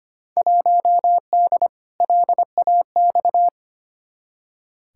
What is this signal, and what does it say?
Morse code, heard as a keyed tone, about 25 wpm, 700 Hz. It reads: 1D LAX
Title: Field Day Exchanges - 25wpm